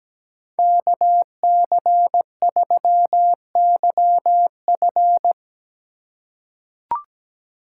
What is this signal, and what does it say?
Morse code, heard as a keyed tone, about 17 wpm, 700 Hz. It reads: KC3YF
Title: USA Call Signs - 5 Characters - 2x Character Spacing - 17wpm